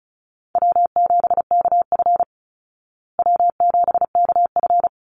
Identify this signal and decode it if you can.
Morse code, heard as a keyed tone, about 34 wpm, 700 Hz. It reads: W7KF W7KF
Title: Easy Call Signs Repeated 3x - 35wpm